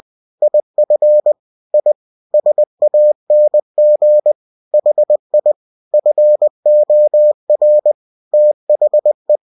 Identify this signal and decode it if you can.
Morse code, heard as a keyed tone, about 20 wpm, 600 Hz. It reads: IF I SANG HI FOR THE